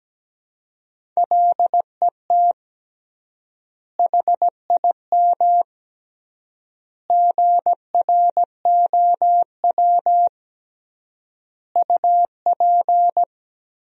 Morse code, heard as a keyed tone, about 17 wpm, 700 Hz. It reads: LET HIM GROW UP
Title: 3x Word-Spacing - Sentences from Top 300 Words - 17wpm